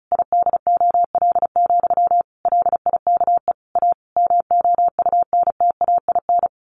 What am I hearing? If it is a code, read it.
Morse code, heard as a keyed tone, about 34 wpm, 700 Hz. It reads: IDOL, LIKE A MOUNTAIN
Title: A Sound Of Thunder 35wpm